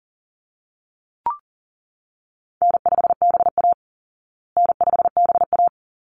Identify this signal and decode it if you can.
Morse code, heard as a keyed tone, about 40 wpm, 700 Hz. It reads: N5BA N5BA
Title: Call Signs Repeated 3x - 40wpm